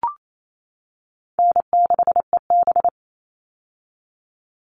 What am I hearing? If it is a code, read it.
Morse code, heard as a keyed tone, about 28 wpm, 700 Hz. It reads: N6EB
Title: All Cut Numbers - 4-Digit 28wpm